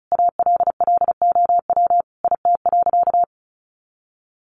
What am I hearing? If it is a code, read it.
Morse code, heard as a keyed tone, about 34 wpm, 700 Hz. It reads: ALLOW IT.
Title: Sentences from Top 700 Words 35wpm